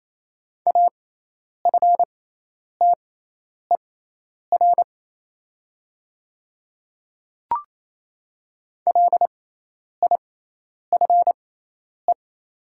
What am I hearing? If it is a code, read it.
Morse code, heard as a keyed tone, about 28 wpm, spaced as for 10 wpm, 700 Hz. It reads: AFTER LIFE
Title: 6x Character Spacing - Top 500 Words - 28wpm